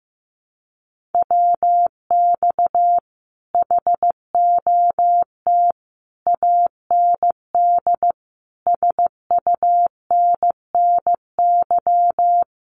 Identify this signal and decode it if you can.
Morse code, heard as a keyed tone, about 15 wpm, 700 Hz. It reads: WX HOT AND SUNNY
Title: Common QSO Phrases Encore 15wpm